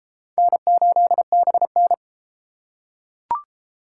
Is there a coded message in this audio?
Yes: Morse code from 0.4 to 1.9 s, heard as a keyed tone, about 30 wpm, 690 Hz. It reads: N8BN